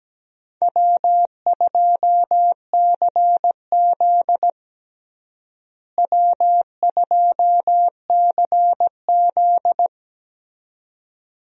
Morse code, heard as a keyed tone, about 17 wpm, 700 Hz. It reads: W2CZ W2CZ
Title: USA Call Signs - 4 Characters - Repeated 2x - 17wpm